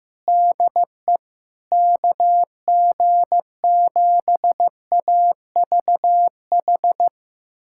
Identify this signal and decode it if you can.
Morse code, heard as a keyed tone, about 15 wpm, 700 Hz. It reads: DE KG7AVH
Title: USA Callsign DE Callsign 15wpm